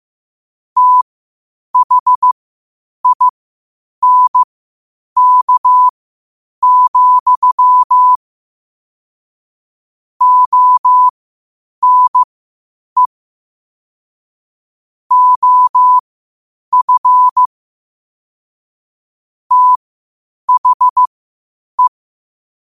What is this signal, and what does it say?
Morse code, heard as a keyed tone, about 15 wpm, spaced as for 8 wpm, 1000 Hz. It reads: THINK, ONE OF THE
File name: qotdfiles / 8wpm